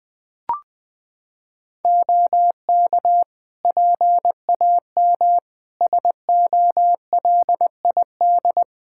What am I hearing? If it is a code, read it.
Morse code, heard as a keyed tone, about 20 wpm, 700 Hz. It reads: OK PAM SOLID
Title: K1USN SST QSO Practice 20wpm